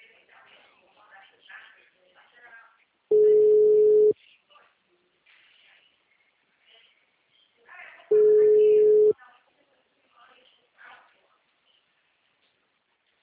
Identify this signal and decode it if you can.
Ringback tone (Europe, 425 Hz), with speech.